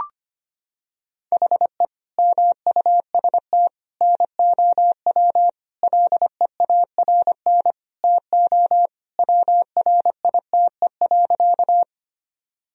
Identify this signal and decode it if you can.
Morse code, heard as a keyed tone, about 25 wpm, 700 Hz. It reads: HE MUST NOW LEARN TO WRITE.
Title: Sentences from Top 1000 Words - Review Part 1 - 25wpm